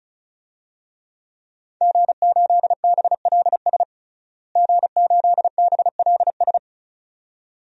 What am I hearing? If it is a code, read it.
Morse code, heard as a keyed tone, about 34 wpm, 700 Hz. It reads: G8BLS G8BLS
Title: DXCC-100 Call Signs - 5 Characters - Repeated 2x - 35wpm